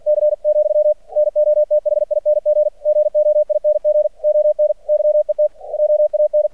Morse code, heard as a keyed tone, about 48 wpm, 600 Hz, with radio noise. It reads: C , NOTHING GOING ON YET OAN
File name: archived / CRSnet